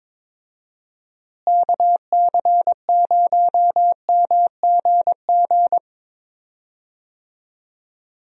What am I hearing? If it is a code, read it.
Morse code, heard as a keyed tone, about 22 wpm, 700 Hz. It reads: KC0MGG
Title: USA Call Signs - 6 Characters - Repeated 2x - 22wpm